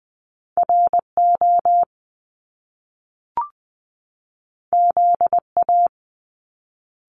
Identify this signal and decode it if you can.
Morse code, heard as a keyed tone, about 20 wpm, 700 Hz. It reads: RO ZA
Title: Top Two Letter Combinations in Words 20wpm